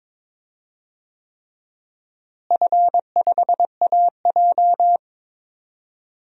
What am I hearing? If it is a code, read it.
Morse code, heard as a keyed tone, about 22 wpm, 700 Hz. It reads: F5AJ
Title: DXCC-100 Call Signs - 4 Characters - Repeated 3x - 22wpm